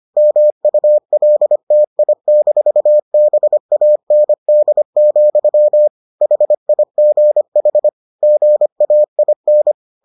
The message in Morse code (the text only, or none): MULTI-BAND, HIGH GAIN